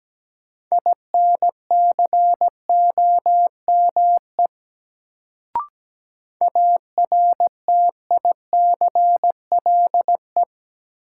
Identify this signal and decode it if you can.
Morse code, heard as a keyed tone, about 17 wpm, 700 Hz. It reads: INCOME ARTICLE
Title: Sentences from Top 1000 Words - Review Part 5 - New Word Drill - 17wpm